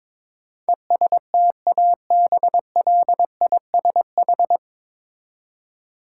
Morse code, heard as a keyed tone, about 22 wpm, 700 Hz. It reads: ESTABLISH
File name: 9 Letter Words - First 3 Letters Given - 22wpm